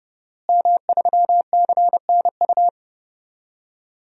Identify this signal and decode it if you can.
Morse code, heard as a keyed tone, about 29 wpm, 700 Hz. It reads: M3CNU
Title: DXCC-100 Call Signs - 5 Characters - Repeated 3x - 30wpm